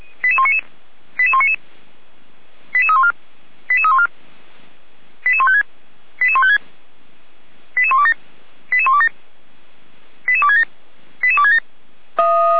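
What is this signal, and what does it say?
Five-tone selective call (ZVEI), heard as quick runs of five tones, tones of about 70 ms. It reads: 88199 88199 88314 88314 88156 88156 88127 88127 88267 88267
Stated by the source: ZVEI five-tone call 88235 received on an SDR, from the Feuerwehr Orlamünde fire brigade server.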